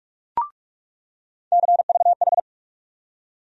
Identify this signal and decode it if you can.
Morse code, heard as a keyed tone, about 43 wpm, 700 Hz. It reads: CVH
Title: USA Call Sign Postfixes - 3 Letters 45wpm